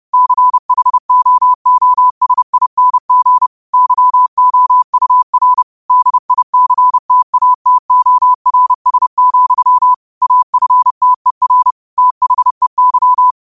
Morse code, heard as a keyed tone, about 31 wpm, 1000 Hz. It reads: CHOOSING YOUR DICTATORS, AFTER THEY
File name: qotdfiles / 30wpm